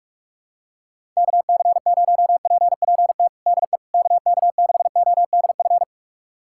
Morse code, heard as a keyed tone, about 46 wpm, 700 Hz. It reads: KC0PPT DE KK6ODF
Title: USA Callsign DE Callsign 45wpm